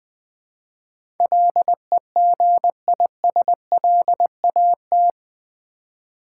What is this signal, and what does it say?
Morse code, heard as a keyed tone, about 20 wpm, 700 Hz. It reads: LEGISLAT
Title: Build A Word - 4 Letters At A Time - 11-Letter Words - 20wpm